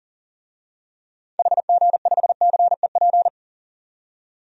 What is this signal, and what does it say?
Morse code, heard as a keyed tone, about 40 wpm, 700 Hz. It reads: HG5CEP
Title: DXCC-100 Call Signs - 6 Characters - 40wpm